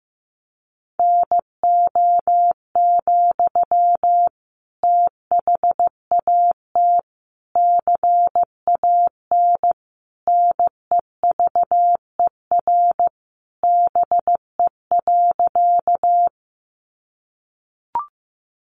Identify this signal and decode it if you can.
Morse code, heard as a keyed tone, about 15 wpm, 700 Hz. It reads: NO, THAT CAN NEVER BE.
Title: Sentences from Top 300 Words 15wpm